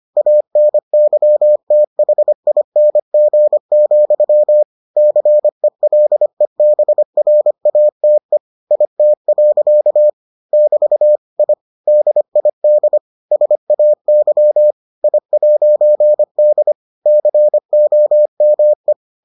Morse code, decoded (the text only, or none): ANYTHING, CELEBRATE IT. <BT> I DID SAY I'D COME